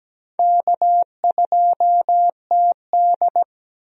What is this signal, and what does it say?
Morse code, heard as a keyed tone, about 17 wpm, 700 Hz. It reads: K2TD
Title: USA Call Signs - 4 Characters - Repeated 2x - 17wpm